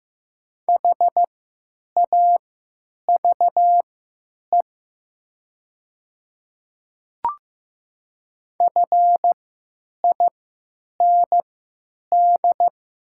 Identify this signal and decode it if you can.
Morse code, heard as a keyed tone, about 15 wpm, spaced as for 8 wpm, 700 Hz. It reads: HAVE FIND
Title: 3x Character Spacing - Top 100 Words - 15wpm